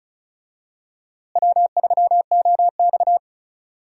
Morse code, heard as a keyed tone, about 36 wpm, 700 Hz. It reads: W3OX
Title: USA Call Signs - 4 Characters - Repeated 2x - 35wpm